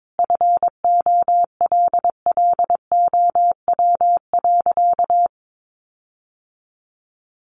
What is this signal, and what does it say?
Morse code, heard as a keyed tone, about 22 wpm, 700 Hz. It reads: FOLLOW.
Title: Sentences from Top 1500 Words 22wpm